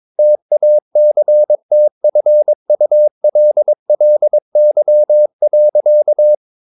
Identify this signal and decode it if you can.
Morse code, heard as a keyed tone, about 22 wpm, 600 Hz. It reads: TACTFULLY.
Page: Romanen Never af Ken Follet på engelsk.